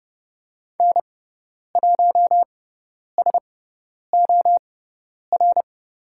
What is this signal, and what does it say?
Morse code, heard as a keyed tone, about 30 wpm, spaced as for 10 wpm, 690 Hz. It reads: N1SOR
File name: Lesson 18g - Letter m - Call Signs 30wpm